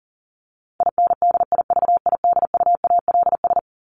Morse code, heard as a keyed tone, about 40 wpm, 700 Hz. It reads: INDIVIDUALS
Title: ES:S Suffix - Top 100 Words with Base Words 40wpm